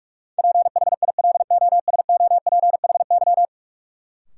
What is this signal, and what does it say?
Morse code, heard as a keyed tone, about 46 wpm, 700 Hz. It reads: PHILOSOPHY
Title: Sentences from Top 900 Words - New Word Drill - 45wpm